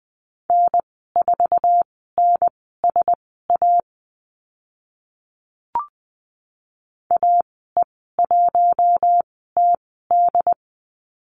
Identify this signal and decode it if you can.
Morse code, heard as a keyed tone, about 20 wpm, spaced as for 15 wpm, 700 Hz. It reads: N4NSA AE1TD
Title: 082 Lesson 14g - Number 2 - Call Signs 20wpm